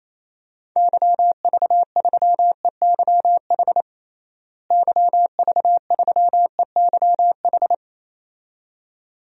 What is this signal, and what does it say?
Morse code, heard as a keyed tone, about 27 wpm, 700 Hz. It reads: YV3EYH YV3EYH
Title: DXCC-100 Call Signs - 6 Characters - Repeated 2x - 28wpm